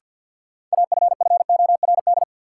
Morse code, heard as a keyed tone, about 52 wpm, 700 Hz. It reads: AFFORD
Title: Sentences from Top 1100 Words - New Word Drill Alt - 50wpm